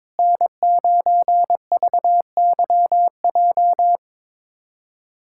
Morse code, heard as a keyed tone, about 22 wpm, 700 Hz. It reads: N9VYJ
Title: USA Callsign DE Callsign - 5 Characters - 22wpm